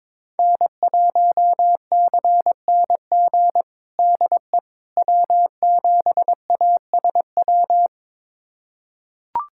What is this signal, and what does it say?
Morse code, heard as a keyed tone, about 22 wpm, 700 Hz. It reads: N1CNG DE W7ASW
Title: USA Callsign DE Callsign - 5 Characters - 22wpm